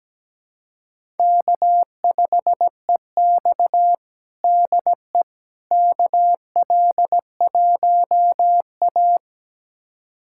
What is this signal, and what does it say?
Morse code, heard as a keyed tone, about 17 wpm, 700 Hz. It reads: K5EX DE KL1A
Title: USA Callsign DE Callsign - 4 Characters - 17wpm